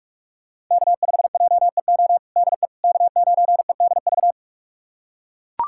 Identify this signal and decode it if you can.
Morse code, heard as a keyed tone, about 42 wpm, 700 Hz. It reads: K5JEO DE K9EDV
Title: USA Callsign DE Callsign - 5 Characters - 45wpm